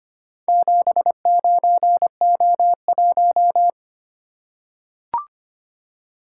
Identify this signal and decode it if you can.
Morse code, heard as a keyed tone, about 25 wpm, 700 Hz. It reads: 79O1
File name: practice / 25wpm / Four Letter-Number Combinations - 25wpm